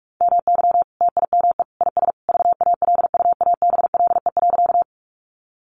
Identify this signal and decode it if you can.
Morse code, heard as a keyed tone, about 44 wpm, 700 Hz. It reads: MY TIME IS VALUABLE.
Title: Sentences from Top 1100 Words 45wpm